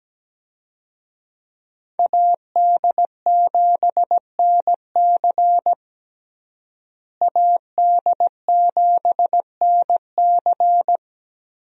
Morse code, heard as a keyed tone, about 17 wpm, 700 Hz. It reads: AD7NC AD7NC
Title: USA Call Signs - 5 Characters - Repeated 3x - 17wpm